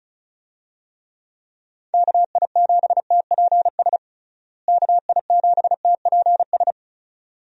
Morse code, heard as a keyed tone, about 34 wpm, 700 Hz. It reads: KI7TPS KI7TPS
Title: USA Call Signs - 6 Characters - Repeated 2x - 35wpm